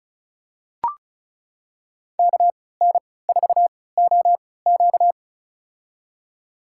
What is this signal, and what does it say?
Morse code, heard as a keyed tone, about 34 wpm, spaced as for 20 wpm, 700 Hz. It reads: KN4OQ
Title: USA Call Signs - 5 Characters - 3x Character Spacing - 35wpm